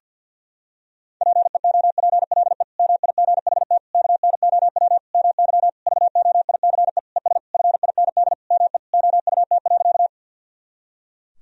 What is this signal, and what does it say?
Morse code, heard as a keyed tone, about 52 wpm, 700 Hz. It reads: PEOPLE MIGHT KNOW MY VOICE ES FIND ME OUT.
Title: Sentences from Top 500 Words and Abbreviations 50wpm